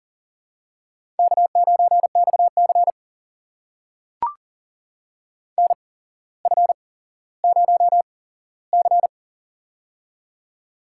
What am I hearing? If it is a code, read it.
Morse code, heard as a keyed tone, about 42 wpm, spaced as for 17 wpm, 700 Hz. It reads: K9XC NF0C
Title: USA Call Signs - 4 Characters - 8x Character Spacing - 40wpm